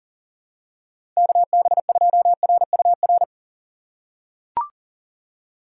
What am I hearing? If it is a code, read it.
Morse code, heard as a keyed tone, about 40 wpm, 700 Hz. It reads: KB2RUR
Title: USA Call Signs - 6 Characters - 4x Character Spacing - 40wpm